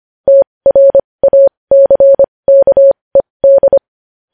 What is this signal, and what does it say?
Morse code, heard as a keyed tone, about 25 wpm, spaced as for 20 wpm, 550 Hz. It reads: TRACKED
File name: podcast / News headlines-1x-25x20